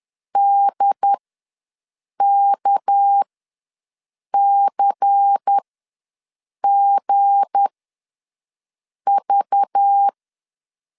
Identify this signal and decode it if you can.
Morse code, heard as a keyed tone, about 10 wpm, spaced as for 6 wpm, 800 Hz. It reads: DKCGV
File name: LEZIONI / 7 GF9Z / 7 G ws 10wpm 3843